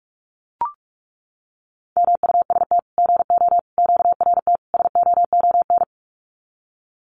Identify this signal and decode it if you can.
Morse code, heard as a keyed tone, about 44 wpm, 700 Hz. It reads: MUST GO QRT SOON
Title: Common QSO Phrases 45wpm